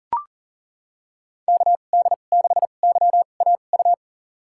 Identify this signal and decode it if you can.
Morse code, heard as a keyed tone, about 37 wpm, spaced as for 28 wpm, 700 Hz. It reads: KD6YAU
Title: USA Call Signs - 2x Character Spacing - 40wpm